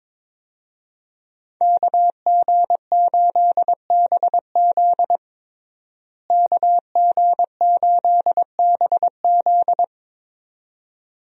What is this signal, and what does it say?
Morse code, heard as a keyed tone, about 22 wpm, 700 Hz. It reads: KG8BZ KG8BZ
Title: USA Call Signs - 5 Characters - Repeated 2x - 22wpm